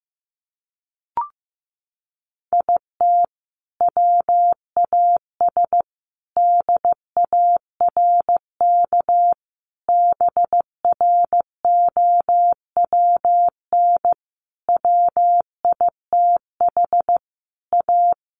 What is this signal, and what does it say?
Morse code, heard as a keyed tone, about 15 wpm, 700 Hz. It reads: IT WAS DARK BROWN WITH A
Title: Sentences from Top 1400 Words 15wpm